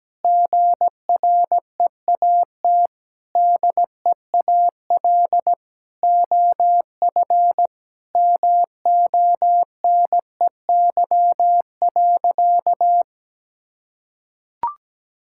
Morse code, heard as a keyed tone, about 17 wpm, 700 Hz. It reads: GREAT DEAL OF MONEY.
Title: Sentences from Top 1000 Words - Review Part 6 - 17wpm